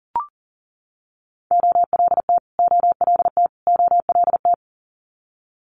Morse code, heard as a keyed tone, about 42 wpm, 700 Hz. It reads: OLT OLT OLT
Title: Multi-letter Phonemes Repeated 3x 40wpm